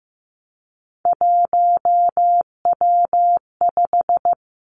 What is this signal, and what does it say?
Morse code, heard as a keyed tone, about 15 wpm, 700 Hz. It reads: 1W5
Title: Three Challenging Letter-Number Combinations 15wpm